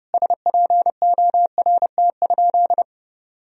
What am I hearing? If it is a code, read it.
Morse code, heard as a keyed tone, about 30 wpm, 700 Hz. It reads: SPORT?
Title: Sentences from Top 1000 Words - Review Part 4 - 30wpm